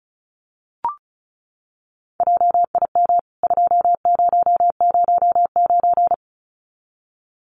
Morse code, heard as a keyed tone, about 34 wpm, 700 Hz. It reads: JIM 2009
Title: CWT Exchange 35wpm